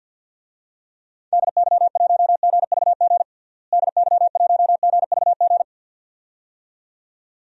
Morse code, heard as a keyed tone, about 48 wpm, 700 Hz. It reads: DY1GVG DY1GVG
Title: DXCC-100 Call Signs - 6 Characters - Repeated 2x - 50wpm